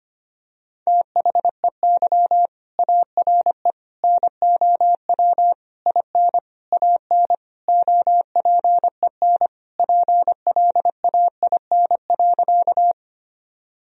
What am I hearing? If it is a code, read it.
Morse code, heard as a keyed tone, about 25 wpm, 700 Hz. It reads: THEY ARE NOW IN AN OPEN PLAIN.
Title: Sentences from Top 1000 Words - Review Part 2 - 25wpm